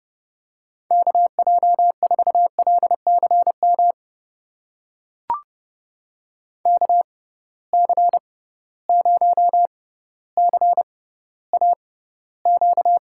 Morse code, heard as a keyed tone, about 30 wpm, spaced as for 15 wpm, 700 Hz. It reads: KJ4LCM KC0CAQ
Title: USA Call Signs - 6 Characters - 6x Character Spacing - 30wpm